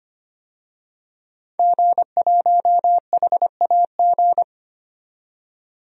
Morse code, heard as a keyed tone, about 25 wpm, 700 Hz. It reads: G1HAG
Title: DXCC-100 Call Signs - 5 Characters - 25wpm